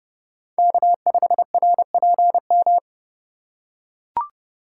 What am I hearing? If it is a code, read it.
Morse code, heard as a keyed tone, about 32 wpm, 700 Hz. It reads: K5RPM
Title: USA Call Signs - 3x Character Spacing - 30wpm